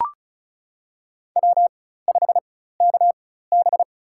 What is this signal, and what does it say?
Morse code, heard as a keyed tone, about 36 wpm, spaced as for 16 wpm, 700 Hz. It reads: W5KB
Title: USA Call Signs - 4 Characters - 4x Character Spacing - 35wpm